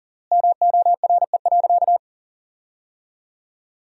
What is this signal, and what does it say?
Morse code, heard as a keyed tone, about 40 wpm, 700 Hz. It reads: MORE.
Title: 3x Word-Spacing - Sentences from Top 500 Words - 40wpm